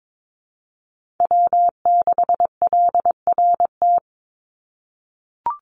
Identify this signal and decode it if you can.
Morse code, heard as a keyed tone, about 22 wpm, 700 Hz. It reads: W6LRT
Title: USA Call Signs 22wpm